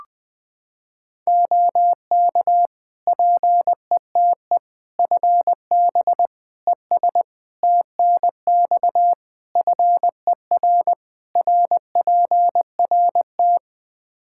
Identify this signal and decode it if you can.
Morse code, heard as a keyed tone, about 20 wpm, 700 Hz. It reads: OK PETE FB ES TNX FER RPRT
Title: Practice QSO - Exchange 2 - Info - 20wpm